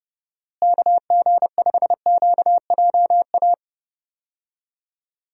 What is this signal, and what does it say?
Morse code, heard as a keyed tone, about 30 wpm, 700 Hz. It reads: KG5QJA
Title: USA Call Signs - 6 Characters - Repeated 3x - 30wpm